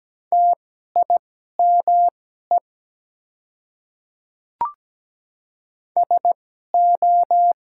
Morse code, heard as a keyed tone, about 17 wpm, spaced as for 12 wpm, 690 Hz. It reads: TIME SO
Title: 2x Character Spacing - Top 100 Words - 17wpm